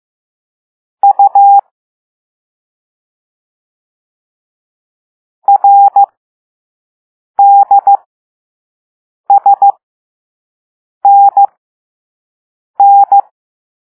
Morse code, heard as a keyed tone, about 15 wpm, spaced as for 5 wpm, 800 Hz. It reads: U RDSNN
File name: Morse / Level6-All-A